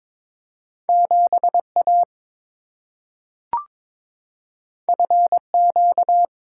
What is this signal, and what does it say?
Morse code, heard as a keyed tone, about 22 wpm, 700 Hz. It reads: 7A FQ
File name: practice / 22wpm / Two Letter-Number Combinations - 22wpm